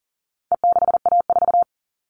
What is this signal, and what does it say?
Morse code, heard as a keyed tone, about 40 wpm, 700 Hz. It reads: E6A4
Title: All Cut Numbers - 4-Digit 40wpm